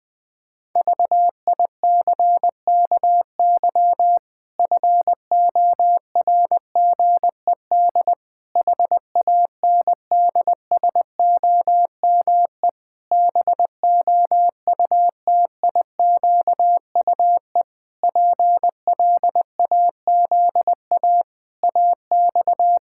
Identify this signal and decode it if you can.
Morse code, heard as a keyed tone, about 21 wpm, 700 Hz. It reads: VICKY FORGED HANDSOME BOUTIQUE PLAZA AX
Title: Pangram Challenge v2 - 20wpm